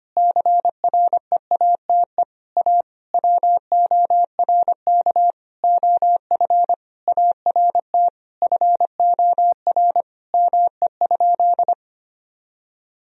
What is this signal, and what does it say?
Morse code, heard as a keyed tone, about 25 wpm, 700 Hz. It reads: CREATE A WORK OF ART FOR ME?
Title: Sentences from Top 1000 Words - Review Part 3 - 25wpm